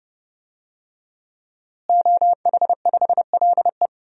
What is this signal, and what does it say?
Morse code, heard as a keyed tone, about 30 wpm, 700 Hz. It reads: OH5LE
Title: DXCC-100 Call Signs - 5 Characters - Repeated 2x - 30wpm